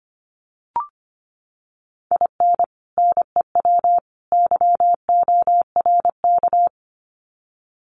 Morse code, heard as a keyed tone, about 25 wpm, 700 Hz. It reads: IN NEW YORK
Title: Sets of 3 Words 25wpm